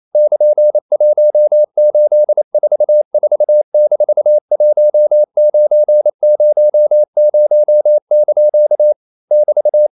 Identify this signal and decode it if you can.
Morse code, heard as a keyed tone, about 28 wpm, 600 Hz. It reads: <KN>1844-1900) <BT>
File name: CW / Practice / EnglishQuotations0128